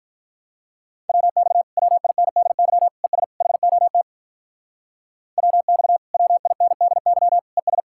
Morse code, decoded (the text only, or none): WX WINDY ES HOT WX WINDY ES